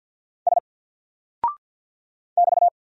Morse code, heard as a keyed tone, about 50 wpm, 700 Hz. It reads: S <BT>
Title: Single Character (Letter:Number:Punctuation:Prosign) 50wpm